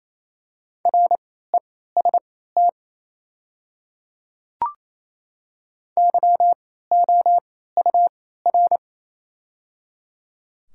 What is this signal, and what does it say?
Morse code, heard as a keyed tone, about 28 wpm, spaced as for 16 wpm, 690 Hz. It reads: REST YOUR
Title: 3x Character Spacing - Top 500 Words - 28wpm